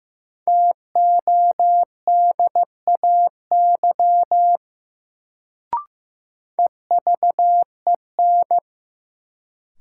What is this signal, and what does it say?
Morse code, heard as a keyed tone, about 15 wpm, 700 Hz. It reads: TODAY EVEN
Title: Sentences from Top 600 Words - New Word Drill - 15wpm